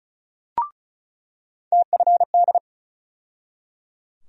Morse code, heard as a keyed tone, about 36 wpm, 700 Hz. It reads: TFD
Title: USA Call Sign Postfixes - 3 Letters 35wpm